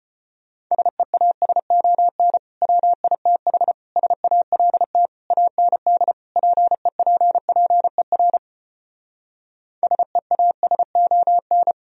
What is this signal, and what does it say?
Morse code, heard as a keyed tone, about 33 wpm, 700 Hz. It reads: SEASON WITH SALT AND PEPPER SEASON